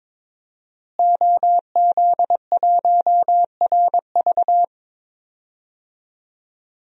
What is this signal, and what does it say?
Morse code, heard as a keyed tone, about 22 wpm, 700 Hz. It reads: OZ1RV
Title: DXCC-100 Call Signs - 5 Characters - Repeated 2x - 22wpm